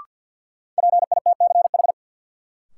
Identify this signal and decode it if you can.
Morse code, heard as a keyed tone, about 53 wpm, 700 Hz. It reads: PITCH
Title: Sentences from Top 1300 Words - New Word Drill Alt - 50wpm